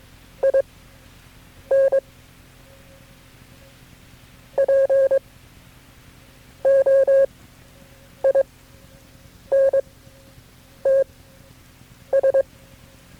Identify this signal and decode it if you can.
Morse code, heard as a keyed tone, about 22 wpm, spaced as for 7 wpm, 540 Hz. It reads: IN POINTS
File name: FreeFiles / CW-practice / KB6MT-tape3-side1